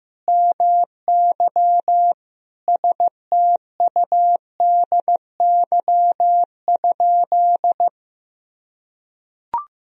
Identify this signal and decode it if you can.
Morse code, heard as a keyed tone, about 15 wpm, 700 Hz. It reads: MY STUDY?
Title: Sentences from Top 1000 Words - Review Part 1 - 15wpm